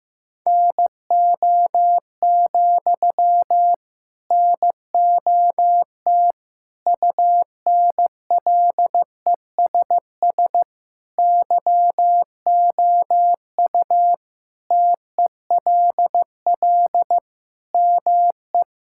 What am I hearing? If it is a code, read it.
Morse code, heard as a keyed tone, about 16 wpm, 700 Hz. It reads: NO, NOT UNLESS YOU TELL ME
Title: Sentences from Top 1000 Words - Review Part 6 - 15wpm